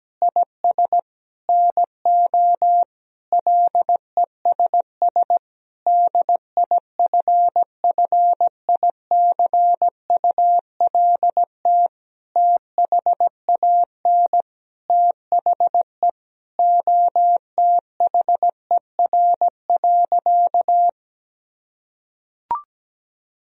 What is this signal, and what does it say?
Morse code, heard as a keyed tone, about 17 wpm, 700 Hz. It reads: IS NO LESS DIFFICULT THAN THE OTHER.
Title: Sentences from Top 1000 Words - Review Part 2 - 17wpm